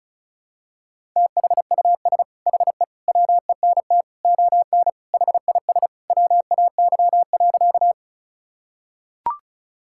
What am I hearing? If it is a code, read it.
Morse code, heard as a keyed tone, about 36 wpm, 700 Hz. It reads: THUS HE WENT ON HIS WAY.
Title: Sentences from Top 1000 Words - Review Part 5 - 35wpm